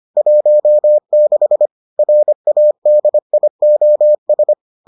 Morse code, heard as a keyed tone, about 25 wpm, 600 Hz. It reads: 16 RADIOS